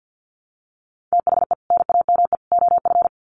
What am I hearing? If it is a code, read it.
Morse code, heard as a keyed tone, about 50 wpm, 700 Hz. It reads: THE NAME OF
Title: Sets of 3 Words 50wpm